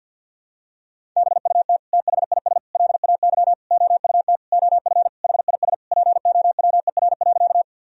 Morse code, heard as a keyed tone, about 50 wpm, 700 Hz. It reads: BUT THIS LAY OUT OF HIS POWER.
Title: Sentences from Top 1000 Words - Review Part 2 - 50wpm